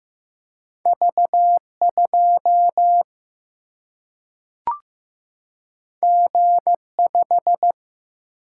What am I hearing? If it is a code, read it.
Morse code, heard as a keyed tone, about 15 wpm, 700 Hz. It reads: V2 G5
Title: Single-Letter Single-Number 15wpm